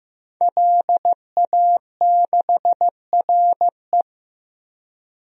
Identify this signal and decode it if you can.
Morse code, heard as a keyed tone, about 15 wpm, 700 Hz. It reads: LA6RE
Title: DXCC-100 Call Signs - 5 Characters - Repeated 3x - 15wpm